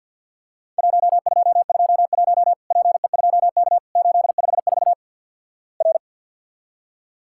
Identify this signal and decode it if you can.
Morse code, heard as a keyed tone, about 48 wpm, 700 Hz. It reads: 1221 PE2K 854 R
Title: Worked All Europe DX Contest - QTC Exchange with Repeats - 50wpm